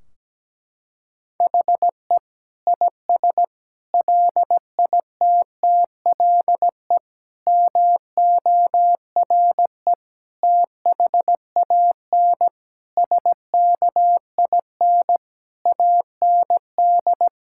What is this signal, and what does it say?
Morse code, heard as a keyed tone, about 17 wpm, 700 Hz. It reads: HE IS LITTLE MORE THAN SKIN AND
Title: Sentences from Top 1600 Words 17wpm